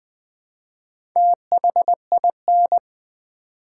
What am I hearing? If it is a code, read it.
Morse code, heard as a keyed tone, about 20 wpm, 700 Hz. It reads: THIN
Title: Build A Word - 1 Letter At A Time - 3 to 5-Letter Words - 20wpm